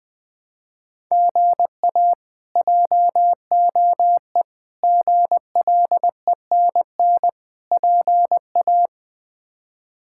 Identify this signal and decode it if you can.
Morse code, heard as a keyed tone, about 21 wpm, 700 Hz. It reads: GA JOE GLENN PA
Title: K1USN SST QSO Practice - 20wpm